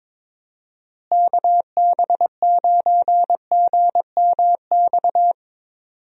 Morse code, heard as a keyed tone, about 21 wpm, 700 Hz. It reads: KB9GMX
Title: USA Call Signs - 6 Characters - Repeated 3x - 22wpm